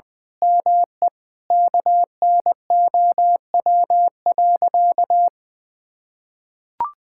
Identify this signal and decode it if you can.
Morse code, heard as a keyed tone, about 21 wpm, 700 Hz. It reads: ME KNOW.
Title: Sentences with 8-Word Phrases - Spoken Beginning - 20wpm